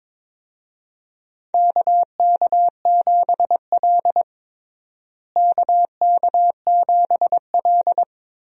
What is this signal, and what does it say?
Morse code, heard as a keyed tone, about 22 wpm, 700 Hz. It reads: KK7L KK7L
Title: USA Call Signs - 4 Characters - Repeated 3x - 22wpm